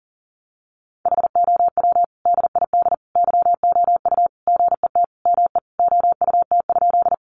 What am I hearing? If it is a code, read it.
Morse code, heard as a keyed tone, about 40 wpm, 700 Hz. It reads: HOW DID YOU GET ME OUT?
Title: Sentences from Top 200 Words 40wpm